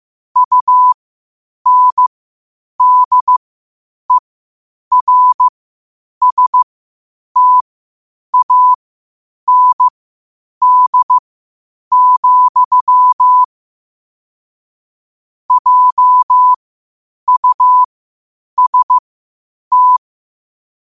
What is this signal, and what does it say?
Morse code, heard as a keyed tone, about 15 wpm, spaced as for 8 wpm, 1000 Hz. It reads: UNDERSTAND, JUST
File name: qotdfiles / 8wpm